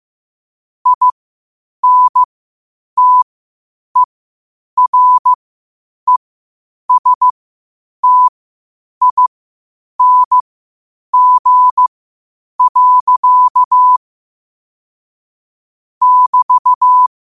Morse code, heard as a keyed tone, about 15 wpm, spaced as for 8 wpm, 1000 Hz. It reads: INTERESTING. <BT>
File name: qotdfiles / 8wpm